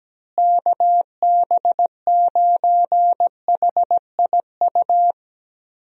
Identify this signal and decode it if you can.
Morse code, heard as a keyed tone, about 17 wpm, 700 Hz. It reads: KB9HIU
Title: USA Call Signs - 6 Characters - Repeated 3x - 17wpm